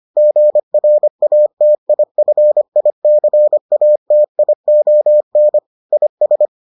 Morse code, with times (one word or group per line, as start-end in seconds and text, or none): 0.2-5.6 GRATIFICATION
5.9-6.5 IS